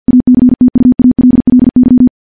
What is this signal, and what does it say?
Morse code, heard as a keyed tone, about 48 wpm, 250 Hz. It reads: ACTUALLY
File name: morse-training-50WPM